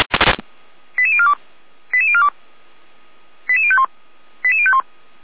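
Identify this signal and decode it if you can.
Five-tone selective call (ZVEI), heard as quick runs of five tones, tones of about 70 ms. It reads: 88042 88042 88051 88051
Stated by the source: ZVEI five-tone selective call received by SDR.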